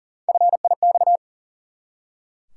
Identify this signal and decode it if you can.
Morse code, heard as a keyed tone, about 40 wpm, 700 Hz. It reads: FIX
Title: Three Letter Words - 40wpm